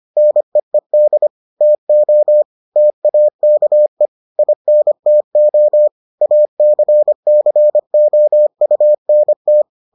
Morse code, heard as a keyed tone, about 25 wpm, 600 Hz. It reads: NEED TO TAKE INTO ACCOUNT